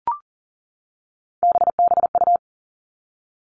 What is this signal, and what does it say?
Morse code, heard as a keyed tone, about 38 wpm, 700 Hz. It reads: BBU
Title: Three Challenging Letter-Number Combinations 40wpm